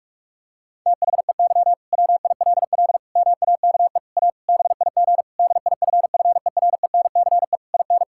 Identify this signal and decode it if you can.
Morse code, heard as a keyed tone, about 46 wpm, 700 Hz. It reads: THEY WILL MAKE A BIG DIFFERENCE IN
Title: Sentences from Top 1000 Words - Review Part 5 - 45wpm